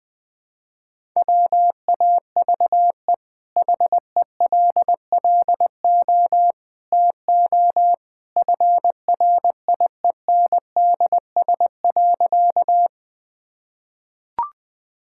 Morse code, heard as a keyed tone, about 20 wpm, 700 Hz. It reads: WAVE HELLO TO FRIENDS.
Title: Easy Sentences with Hints - 20wpm